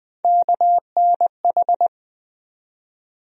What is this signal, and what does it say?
Morse code, heard as a keyed tone, about 20 wpm, 700 Hz. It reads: KNH
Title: USA Call Sign Postfixes - 3 Letters 20wpm